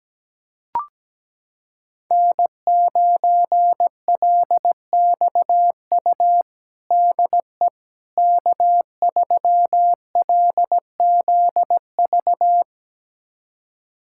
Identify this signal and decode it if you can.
Morse code, heard as a keyed tone, about 17 wpm, 700 Hz. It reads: N9LXU DE K3LZV
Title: USA Callsign DE Callsign - 5 Characters - 17wpm